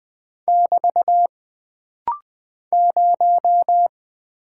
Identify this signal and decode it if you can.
Morse code, heard as a keyed tone, about 20 wpm, 700 Hz. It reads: <BT> 0
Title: Single Character (Letter:Number:Punctuation:Prosign) - Rapid-Fire - 20wpm